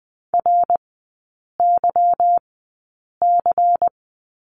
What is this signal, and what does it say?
Morse code, heard as a keyed tone, about 20 wpm, spaced as for 8 wpm, 690 Hz. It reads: RYC
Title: 145 Lesson 24e - Letter y - Three Characters 20wpm